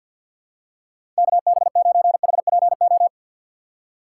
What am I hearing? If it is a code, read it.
Morse code, heard as a keyed tone, about 51 wpm, 700 Hz. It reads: KB9HPO
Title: USA Call Signs - 6 Characters - 4x Character Spacing - 50wpm